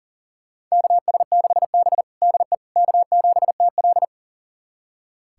Morse code, heard as a keyed tone, about 37 wpm, 700 Hz. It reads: KS6B DE K7TL
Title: USA Callsign DE Callsign - 4 Characters - 40wpm